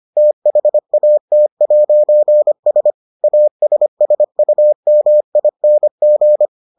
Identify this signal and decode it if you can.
Morse code, heard as a keyed tone, about 25 wpm, 600 Hz. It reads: THAT'S ASSUMING